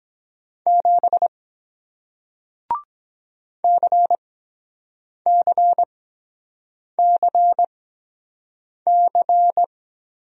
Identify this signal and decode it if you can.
Morse code, heard as a keyed tone, about 23 wpm, spaced as for 7 wpm, 700 Hz. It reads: 7 CCCC